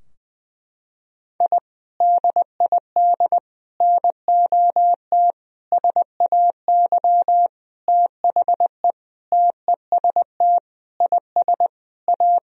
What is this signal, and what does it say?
Morse code, heard as a keyed tone, about 20 wpm, 700 Hz. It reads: I DID NOT SAY THE TEST IS A
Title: Sentences from Top 1000 Words - Review Part 3 - 20wpm